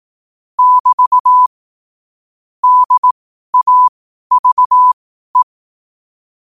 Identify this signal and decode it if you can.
Morse code, heard as a keyed tone, about 18 wpm, spaced as for 12 wpm, 1000 Hz. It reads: <BT> DAVE
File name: qotdfiles / 12wpm